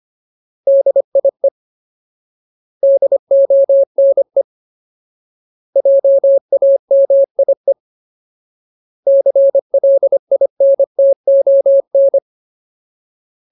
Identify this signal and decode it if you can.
Morse code, heard as a keyed tone, about 25 wpm, 550 Hz. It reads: DIE DONE JAMIE CLINTON
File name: Ultimate Word List / 25WPM Ultimate